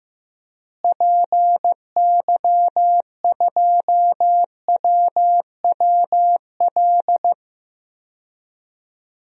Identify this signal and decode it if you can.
Morse code, heard as a keyed tone, about 15 wpm, 700 Hz. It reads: PY2WWL
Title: DXCC-100 Call Signs - 6 Characters - Repeated 2x - 15wpm